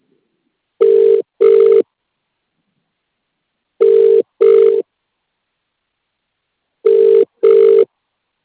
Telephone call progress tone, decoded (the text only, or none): ringback tone